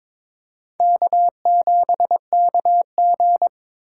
Morse code, heard as a keyed tone, about 22 wpm, 700 Hz. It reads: K7KG
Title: USA Call Signs - 4 Characters - 2x Character Spacing - 22wpm